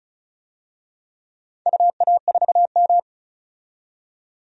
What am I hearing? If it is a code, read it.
Morse code, heard as a keyed tone, about 35 wpm, 700 Hz. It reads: UA4M
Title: DXCC-100 Call Signs - 4 Characters - 35wpm